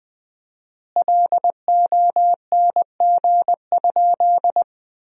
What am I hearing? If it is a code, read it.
Morse code, heard as a keyed tone, about 20 wpm, 700 Hz. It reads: LONG?
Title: 4x Word-Spacing - Sentences from Top 500 Words - 20wpm